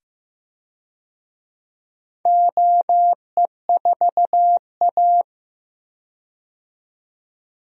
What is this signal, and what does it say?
Morse code, heard as a keyed tone, about 15 wpm, 700 Hz. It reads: OE4A
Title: DXCC-100 Call Signs - 4 Characters - 15wpm